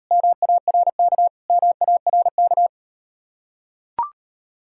Morse code, heard as a keyed tone, about 38 wpm, 700 Hz. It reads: MARK MARK